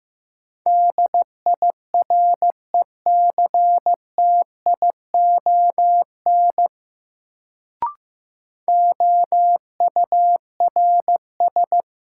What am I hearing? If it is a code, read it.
Morse code, heard as a keyed tone, about 15 wpm, 700 Hz. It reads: DIRECTION OURS
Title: Sentences from Top 1000 Words - Review Part 5 - New Word Drill - 15wpm